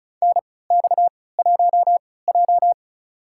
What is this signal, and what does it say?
Morse code, heard as a keyed tone, about 34 wpm, spaced as for 20 wpm, 700 Hz. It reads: NX1J
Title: USA Call Signs - 3x Character Spacing - 35wpm